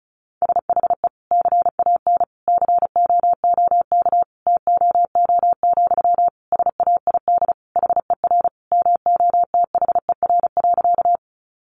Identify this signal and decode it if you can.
Morse code, heard as a keyed tone, about 34 wpm, 700 Hz. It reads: SHE CAN COOK TOO, SAID HER MOTHER.
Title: Sentences from Top 700 Words 35wpm